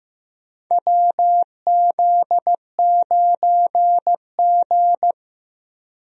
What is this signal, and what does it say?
Morse code, heard as a keyed tone, about 15 wpm, 700 Hz. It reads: WZ9G
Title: USA Call Signs - 4 Characters - Repeated 3x - 15wpm